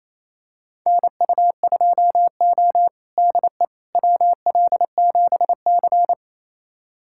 Morse code, heard as a keyed tone, about 28 wpm, 700 Hz. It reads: NU2O DE WL7C
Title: USA Callsign DE Callsign - 4 Characters - 28wpm